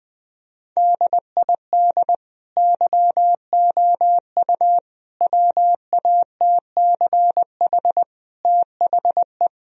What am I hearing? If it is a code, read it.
Morse code, heard as a keyed tone, about 21 wpm, 700 Hz. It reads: DID YOU WATCH THE
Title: Sentences from Top 1000 Words - Review Part 4 - 20wpm